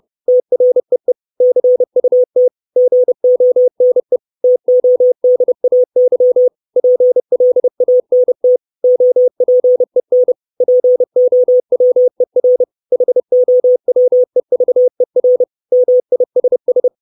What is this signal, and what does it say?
Morse code, heard as a keyed tone, about 30 wpm, 500 Hz. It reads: TREE CUT GONE TODAY PLANT OPEN POWER HOWEVER MISS
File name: 632 Words 30WPM